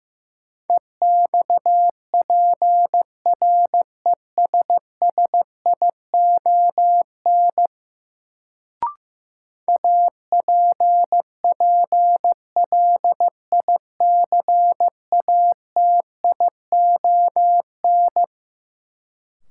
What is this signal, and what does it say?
Morse code, heard as a keyed tone, about 15 wpm, 700 Hz. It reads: EXPRESSION APPLICATION
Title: Sentences from Top 1200 Words - New Word Drill - 15wpm